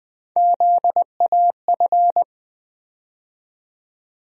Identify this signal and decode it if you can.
Morse code, heard as a keyed tone, about 20 wpm, 700 Hz. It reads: ZAF
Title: USA Call Sign Postfixes - 3 Letters 20wpm